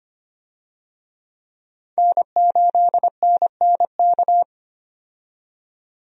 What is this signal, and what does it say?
Morse code, heard as a keyed tone, about 25 wpm, 700 Hz. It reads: N8NNK
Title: USA Call Signs - 5 Characters - 25wpm